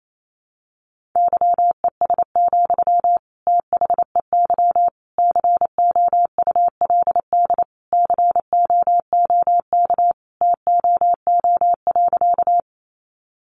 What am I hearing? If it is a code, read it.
Morse code, heard as a keyed tone, about 28 wpm, 700 Hz. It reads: YES, THEY COULD COOK TOO.
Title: Sentences from Top 700 Words 28wpm